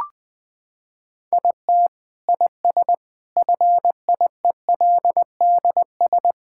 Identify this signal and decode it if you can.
Morse code, heard as a keyed tone, about 20 wpm, 700 Hz. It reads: IT IS FIELDS
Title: Sentences with 7-Word Phrases - 20wpm